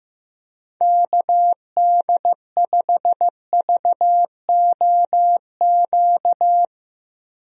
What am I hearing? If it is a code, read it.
Morse code, heard as a keyed tone, about 15 wpm, 700 Hz. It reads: KD5VOQ
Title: USA Call Signs - 6 Characters - Repeated 2x - 15wpm